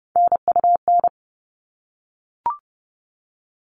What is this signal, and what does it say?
Morse code, heard as a keyed tone, about 30 wpm, 700 Hz. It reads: NUN
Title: All Cut Numbers - 3-Digit 30wpm